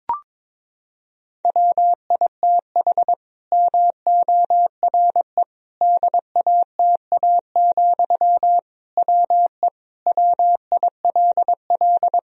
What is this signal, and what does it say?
Morse code, heard as a keyed tone, about 22 wpm, 700 Hz. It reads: WITH MORE DATA, WE WILL
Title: Sentences from Top 1000 Words - Review Part 3 - 22wpm